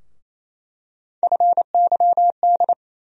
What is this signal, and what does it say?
Morse code, heard as a keyed tone, about 28 wpm, 700 Hz. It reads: FYD
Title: USA Call Sign Postfixes - 3 Letters 28wpm